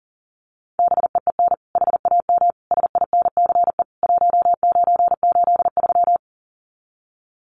Morse code, heard as a keyed tone, about 40 wpm, 700 Hz. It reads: BEEN HAM SINCE 1983
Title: Common QSO Phrases 40wpm